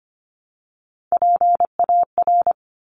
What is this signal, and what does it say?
Morse code, heard as a keyed tone, about 25 wpm, 700 Hz. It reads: PAR
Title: Top Three Letter Combinations in Words 25wpm